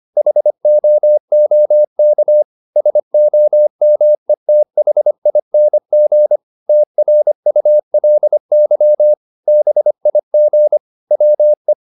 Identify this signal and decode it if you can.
Morse code, heard as a keyed tone, about 25 wpm, 600 Hz. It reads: HOOK SOMETHING TRULY BIG WE
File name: the old man and the sea part 1 25wpm